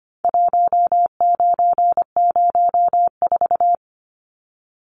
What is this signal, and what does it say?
Morse code, heard as a keyed tone, about 25 wpm, 700 Hz. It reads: 1904
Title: Numbers - 4-Digit 25wpm